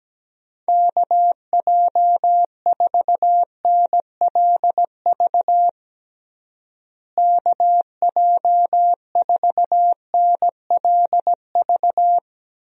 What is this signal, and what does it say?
Morse code, heard as a keyed tone, about 17 wpm, 700 Hz. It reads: KJ4NLV KJ4NLV
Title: USA Call Signs - 6 Characters - Repeated 2x - 17wpm